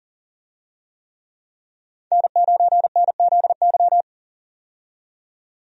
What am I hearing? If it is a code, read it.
Morse code, heard as a keyed tone, about 40 wpm, 700 Hz. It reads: N9NZY
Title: USA Call Signs - 5 Characters - 40wpm